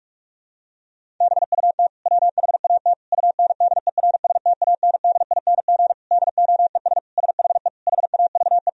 Morse code, heard as a keyed tone, about 46 wpm, 700 Hz. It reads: BUT WHAT UNDERSTANDING DOES SHE HAVE
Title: Sentences from Top 1000 Words - Review Part 3 - 45wpm